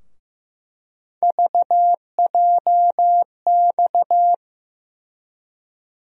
Morse code, heard as a keyed tone, about 15 wpm, 700 Hz. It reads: VJX
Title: USA Call Sign Postfixes - 3 Letters 15wpm